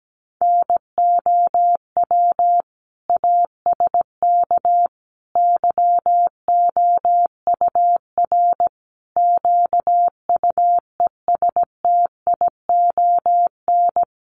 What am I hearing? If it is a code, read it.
Morse code, heard as a keyed tone, about 17 wpm, 700 Hz. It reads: NOW ASK YOUR QUESTION
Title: Sentences from Top 1100 Words 17wpm